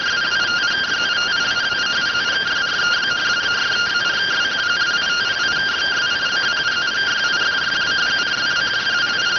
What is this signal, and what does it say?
Two-tone teleprinter signal, about 170 Hz shift.